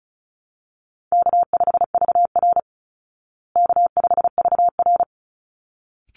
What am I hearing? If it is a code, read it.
Morse code, heard as a keyed tone, about 34 wpm, 700 Hz. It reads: K5VR K5VR
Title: Call Signs Repeated 3x - 35wpm